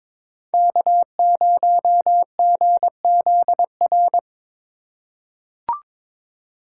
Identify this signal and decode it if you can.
Morse code, heard as a keyed tone, about 22 wpm, 700 Hz. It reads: K0GZR
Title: USA Call Signs - 5 Characters - 3x Character Spacing - 22wpm